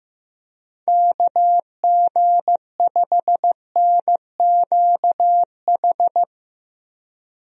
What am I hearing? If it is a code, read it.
Morse code, heard as a keyed tone, about 15 wpm, 700 Hz. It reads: KG5NQH
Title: USA Call Signs - 6 Characters - Repeated 2x - 15wpm